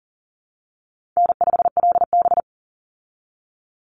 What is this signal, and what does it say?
Morse code, heard as a keyed tone, about 40 wpm, 700 Hz. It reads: N5LB
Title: Easy Call Signs Repeated 2x - 40wpm